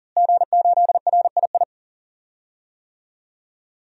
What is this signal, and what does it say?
Morse code, heard as a keyed tone, about 38 wpm, 700 Hz. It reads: G8RII
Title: DXCC-100 Call Signs - 5 Characters - Repeated 3x - 40wpm